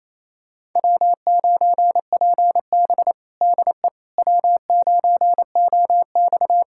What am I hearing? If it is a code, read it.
Morse code, heard as a keyed tone, about 28 wpm, 700 Hz. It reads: W9PB DE W9OX
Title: USA Callsign DE Callsign - 4 Characters - 28wpm